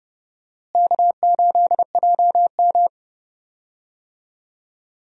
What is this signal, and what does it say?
Morse code, heard as a keyed tone, about 30 wpm, 700 Hz. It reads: K8JM
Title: USA Call Signs - 4 Characters - Repeated 3x - 30wpm